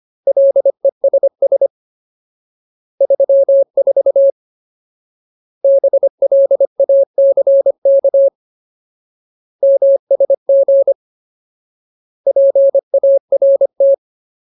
Morse code, heard as a keyed tone, about 25 wpm, 550 Hz. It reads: LESS 34 BLACK MSG PART
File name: Ultimate Word List / 25WPM Ultimate